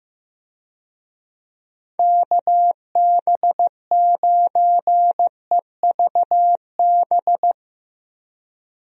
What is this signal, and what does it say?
Morse code, heard as a keyed tone, about 16 wpm, 700 Hz. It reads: KB9EVB
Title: USA Call Signs - 6 Characters - Repeated 3x - 15wpm